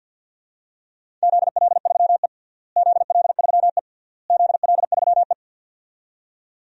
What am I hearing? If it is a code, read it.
Morse code, heard as a keyed tone, about 52 wpm, 700 Hz. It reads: ZL3E ZL3E ZL3E
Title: DXCC-100 Call Signs - 4 Characters - Repeated 3x - 50wpm